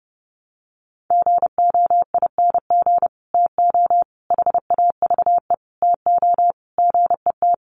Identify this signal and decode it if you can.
Morse code, heard as a keyed tone, about 30 wpm, 700 Hz. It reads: GOING TO HAVE TO GET
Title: Sets of 5 Words Encore2 30wpm